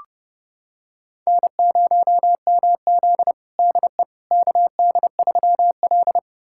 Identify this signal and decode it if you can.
Morse code, heard as a keyed tone, about 32 wpm, 700 Hz. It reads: N0MZ DE KD3L
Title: USA Callsign DE Callsign - 4 Characters - 30wpm